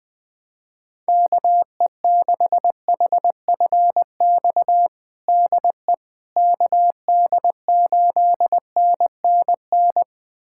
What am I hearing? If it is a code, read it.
Morse code, heard as a keyed tone, about 20 wpm, 700 Hz. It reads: KE6HFX DE KD8NNN
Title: USA Callsign DE Callsign - 6 Characters - 20wpm